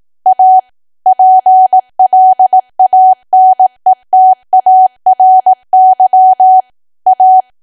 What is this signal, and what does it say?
Morse code, heard as a keyed tone, about 18 wpm, 750 Hz. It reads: A PLANETARY A
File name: cwa / W1AW-Level2 / 18WPM